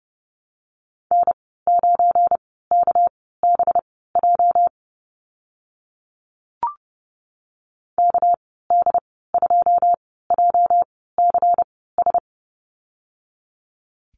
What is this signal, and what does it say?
Morse code, heard as a keyed tone, about 30 wpm, spaced as for 17 wpm, 700 Hz. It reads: N9KBJ KD2JCS
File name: Lesson 37g - Letter x - Call Signs 30wpm